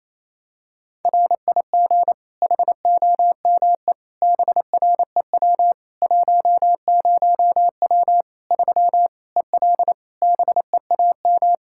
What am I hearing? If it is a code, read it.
Morse code, heard as a keyed tone, about 28 wpm, 700 Hz. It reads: RIG HOME BREW 10W 3 EL BEAM
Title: K1USN SST QSO Practice 28wpm